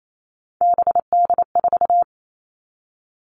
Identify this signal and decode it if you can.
Morse code, heard as a keyed tone, about 28 wpm, 700 Hz. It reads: BD4
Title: Callsign Prefixes for Top 100 DXCC - 28wpm